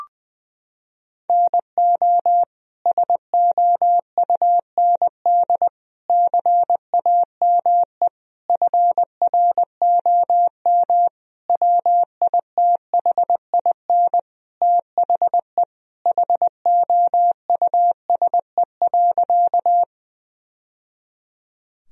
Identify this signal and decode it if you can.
Morse code, heard as a keyed tone, about 20 wpm, 700 Hz. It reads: NO SOUND CAME FROM WITHIN THE HOUSE.
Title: Sentences from Top 1000 Words - Review Part 4 - 20wpm